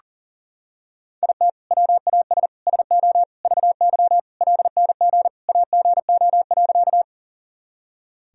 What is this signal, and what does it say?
Morse code, heard as a keyed tone, about 40 wpm, 700 Hz. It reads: IT WAS SO VY LNG AGO.
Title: Sentences from Top 500 Words and Abbreviations 40wpm